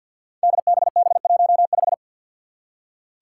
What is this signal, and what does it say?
Morse code, heard as a keyed tone, about 47 wpm, 700 Hz. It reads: DBB15
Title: Five Challenging Letter-Number Combinations 50wpm